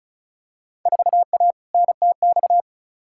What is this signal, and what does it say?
Morse code, heard as a keyed tone, about 36 wpm, 700 Hz. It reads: 4A NTX
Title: Field Day Exchanges - 35wpm